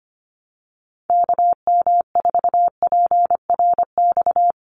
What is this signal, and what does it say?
Morse code, heard as a keyed tone, about 25 wpm, 700 Hz. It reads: KM4PRX
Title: USA Call Signs 25wpm